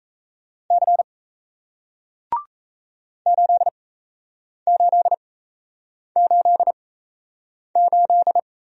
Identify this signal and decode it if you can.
Morse code, heard as a keyed tone, about 37 wpm, spaced as for 8 wpm, 700 Hz. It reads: C 8888